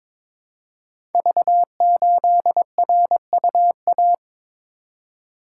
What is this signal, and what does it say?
Morse code, heard as a keyed tone, about 22 wpm, 700 Hz. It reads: V8RUA
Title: Code Groups - 5 Letter:Numbers - 22wpm